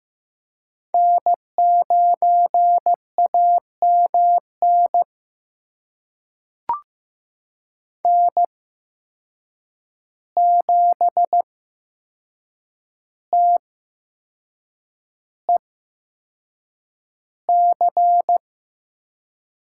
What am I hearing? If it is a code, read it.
Morse code, heard as a keyed tone, about 15 wpm, spaced as for 6 wpm, 700 Hz. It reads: N9AMN N7TEC
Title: USA Call Signs - 5 Characters - 8x Character Spacing - 15wpm